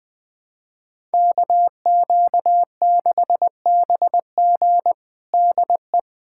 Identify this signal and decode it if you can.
Morse code, heard as a keyed tone, about 20 wpm, 700 Hz. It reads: KQ6BG DE
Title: USA Callsign DE Callsign - 5 Characters - 20wpm